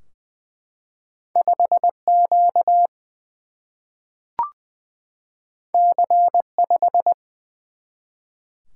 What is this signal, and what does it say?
Morse code, heard as a keyed tone, about 20 wpm, 700 Hz. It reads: 5Q C5
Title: Single-Letter Single-Number 20wpm